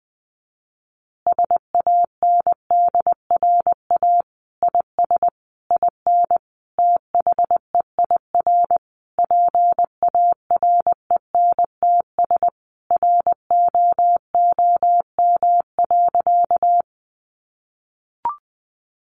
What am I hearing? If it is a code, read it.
Morse code, heard as a keyed tone, about 20 wpm, 700 Hz. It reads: SANDRA IS IN THEIR PARENTS ROOM.
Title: Easy Phrases 20wpm